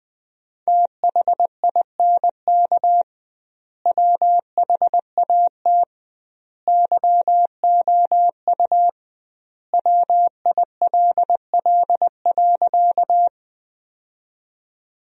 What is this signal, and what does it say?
Morse code, heard as a keyed tone, about 20 wpm, 700 Hz. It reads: THINK WHAT YOU WILL.
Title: 2x Word-Spacing - Sentences from Top 200 Words - 20wpm